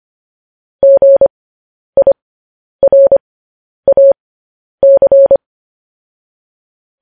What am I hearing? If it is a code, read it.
Morse code, heard as a keyed tone, about 25 wpm, spaced as for 10 wpm, 550 Hz. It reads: GIRAC